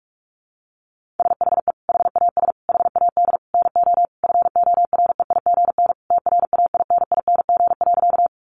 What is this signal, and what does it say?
Morse code, heard as a keyed tone, about 44 wpm, 700 Hz. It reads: SHE HAS HAD NO FOREIGN TRAINING.
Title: Sentences from Top 1400 Words 45wpm